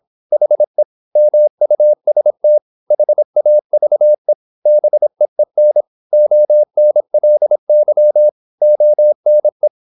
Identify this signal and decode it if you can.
Morse code, heard as a keyed tone, about 26 wpm, 600 Hz. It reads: HE MUST HAVE BEEN ONLY ONE
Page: Never 26wpm | CW med Gnister